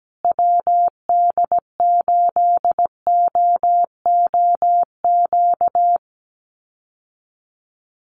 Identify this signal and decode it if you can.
Morse code, heard as a keyed tone, about 17 wpm, 700 Hz. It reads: WD8OOQ
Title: USA Call Signs Repeated 2x 17wpm